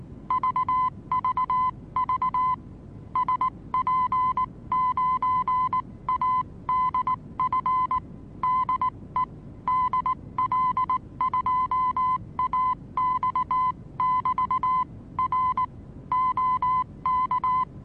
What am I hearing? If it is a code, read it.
Morse code, heard as a keyed tone, about 19 wpm, 1020 Hz. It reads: VVV SP9ADF DE DL2AX<BT> R OK
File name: repos / Morsekurs / DJ4UF Morsen26